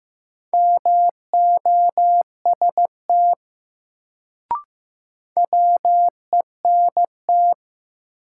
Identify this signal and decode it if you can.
Morse code, heard as a keyed tone, about 15 wpm, 700 Hz. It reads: MOST WENT
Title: Sentences from Top 200 Words - New Word Drill - 15wpm